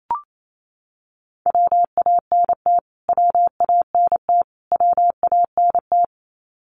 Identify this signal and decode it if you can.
Morse code, heard as a keyed tone, about 28 wpm, 700 Hz. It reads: WANT WANT WANT
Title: Top 300 Words Repeated 28wpm